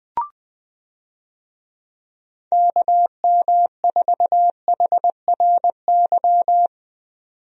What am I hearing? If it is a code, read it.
Morse code, heard as a keyed tone, about 21 wpm, 700 Hz. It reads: KM4HRY
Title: USA Call Signs - 6 Characters - 20wpm